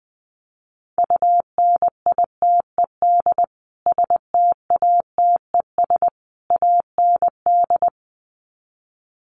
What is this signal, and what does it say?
Morse code, heard as a keyed tone, about 20 wpm, 700 Hz. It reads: UNITED STATES AND
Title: Sets of 3 Words 20wpm